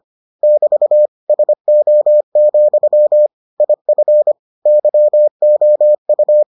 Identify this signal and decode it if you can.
Morse code, heard as a keyed tone, about 25 wpm, 600 Hz. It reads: <BT> SO, IF YOU